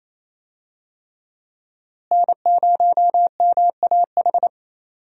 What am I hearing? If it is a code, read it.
Morse code, heard as a keyed tone, about 28 wpm, 700 Hz. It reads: N0MAH
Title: USA Call Signs - 5 Characters - Repeated 3x - 28wpm